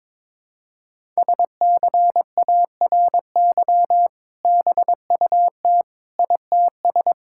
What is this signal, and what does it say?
Morse code, heard as a keyed tone, about 22 wpm, 700 Hz. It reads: SCARY BUT ITS
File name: practice / 22wpm / Sentences with 2-Word Phrases - 22wpm